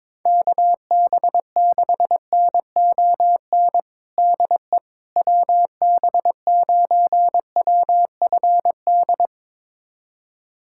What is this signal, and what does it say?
Morse code, heard as a keyed tone, about 22 wpm, 700 Hz. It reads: KB6NON DE WB9WFD
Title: USA Callsign DE Callsign 22wpm